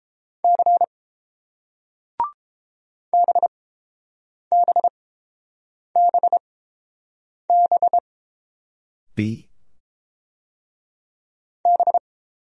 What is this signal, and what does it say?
Morse code, heard as a keyed tone, about 31 wpm, spaced as for 7 wpm, 690 Hz. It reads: C BBBB B